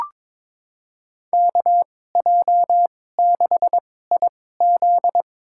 Morse code, heard as a keyed tone, about 22 wpm, spaced as for 16 wpm, 700 Hz. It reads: KJ6IZ
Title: USA Call Signs - 5 Characters - 2x Character Spacing - 22wpm